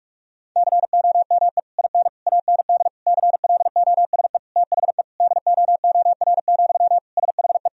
Morse code, heard as a keyed tone, about 46 wpm, 700 Hz. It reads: COME IN AND CLOSE THE DOOR, SHE
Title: Sentences from Top 1000 Words - Review Part 1 - 45wpm